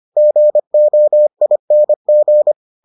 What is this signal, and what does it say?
Morse code, heard as a keyed tone, about 25 wpm, 600 Hz. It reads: GOING